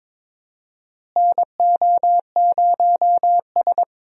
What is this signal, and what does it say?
Morse code, heard as a keyed tone, about 22 wpm, 700 Hz. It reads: NO0S
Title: USA Call Signs - 4 Characters - Repeated 3x - 22wpm